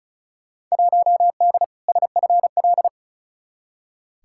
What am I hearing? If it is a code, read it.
Morse code, heard as a keyed tone, about 34 wpm, 700 Hz. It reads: 1D SFL
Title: Field Day Exchanges - 35wpm